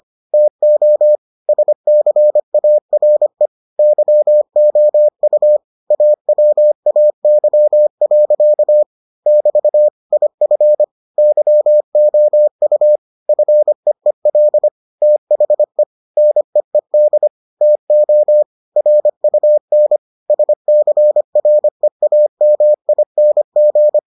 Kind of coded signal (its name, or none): Morse code